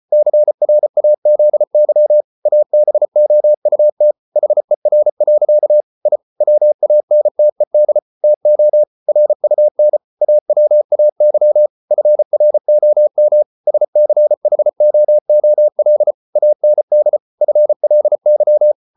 Morse code, heard as a keyed tone, about 34 wpm, 600 Hz. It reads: CRAZY ABOUT HER. I WANTED TO RUN AWAY FROM SCHOOL AND FLY